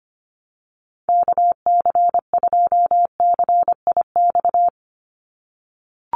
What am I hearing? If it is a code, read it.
Morse code, heard as a keyed tone, about 25 wpm, 700 Hz. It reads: KC2CIX
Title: USA Call Signs 25wpm